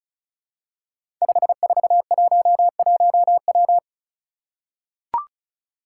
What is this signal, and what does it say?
Morse code, heard as a keyed tone, about 36 wpm, 700 Hz. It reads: 5411W
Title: Five Challenging Letter-Number Combinations 35wpm